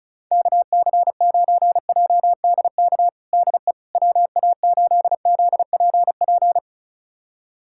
Morse code, heard as a keyed tone, about 34 wpm, 690 Hz. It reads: KC9JDK DE WA8ZPP
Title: USA Callsign DE Callsign - 6 Characters - 35wpm